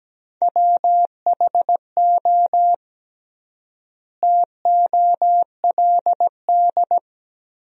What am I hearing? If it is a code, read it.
Morse code, heard as a keyed tone, about 17 wpm, 700 Hz. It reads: WHO TOLD
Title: 3x Word-Spacing - Sentences from Top 400 Words - 17wpm